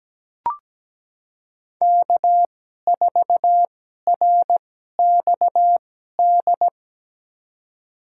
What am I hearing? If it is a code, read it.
Morse code, heard as a keyed tone, about 17 wpm, spaced as for 12 wpm, 700 Hz. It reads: K4RXD
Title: USA Call Signs - 5 Characters - 2x Character Spacing - 17wpm